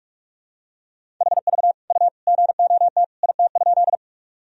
Morse code, heard as a keyed tone, about 43 wpm, 700 Hz. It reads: HV U GOT IT?
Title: Sentences from Top 300 Words and Abbreviations 45wpm